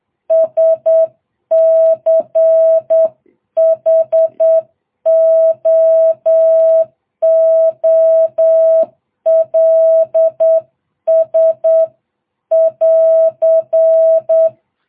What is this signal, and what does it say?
Morse code, heard as a keyed tone, about 8 wpm, 660 Hz. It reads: SCHOOLS <AR>